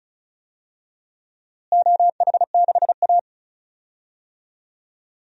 Morse code, heard as a keyed tone, about 36 wpm, 700 Hz. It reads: OH6A
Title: DXCC-100 Call Signs - 4 Characters - Repeated 2x - 35wpm